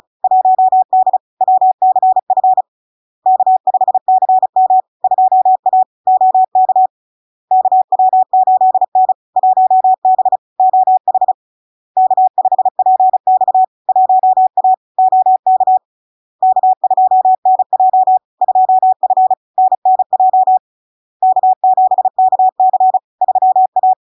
Morse code, heard as a keyed tone, about 36 wpm, 750 Hz. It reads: 1D WCF K5CM 2A OK KW8N 1B OH K5PX 1A OK K2NJ 2F NNJ K7KC 3A
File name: LARK-FD-2022-Log-in-CW-35wpm-750-Hz